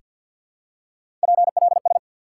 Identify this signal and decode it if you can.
Morse code, heard as a keyed tone, about 47 wpm, 700 Hz. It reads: PLS
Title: Top 180 QSO Elements 50wpm